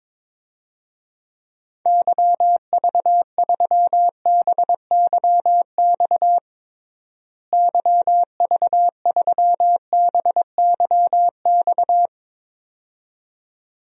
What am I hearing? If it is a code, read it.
Morse code, heard as a keyed tone, about 22 wpm, 700 Hz. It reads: YV3BYX YV3BYX
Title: DXCC-100 Call Signs - 6 Characters - Repeated 2x - 22wpm